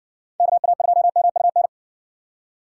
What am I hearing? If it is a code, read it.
Morse code, heard as a keyed tone, about 61 wpm, 700 Hz. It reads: KA2MUN